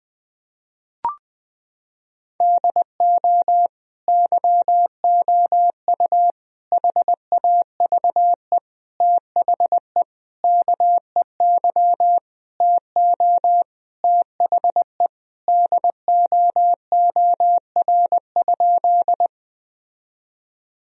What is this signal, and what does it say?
Morse code, heard as a keyed tone, about 20 wpm, 700 Hz. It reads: DO YOU HAVE THE KEY TO THE DOOR?
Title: Sentences from Top 1000 Words - Review Part 4 - 20wpm